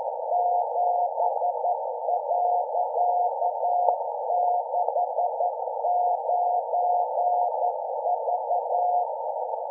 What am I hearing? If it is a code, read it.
Morse code, heard as a keyed tone, about 11 wpm, 710 Hz, with noise. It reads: 7.69V
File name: lu4aao / HDSDR Hz AF